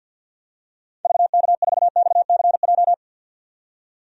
Morse code, heard as a keyed tone, about 52 wpm, 700 Hz. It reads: VK4XCJ
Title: DXCC-100 Call Signs - 6 Characters - 50wpm